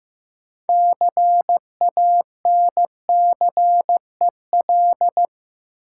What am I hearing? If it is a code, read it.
Morse code, heard as a keyed tone, about 15 wpm, 700 Hz. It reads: CANCEL
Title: Sentences from Top 1200 Words - New Word Drill - 15wpm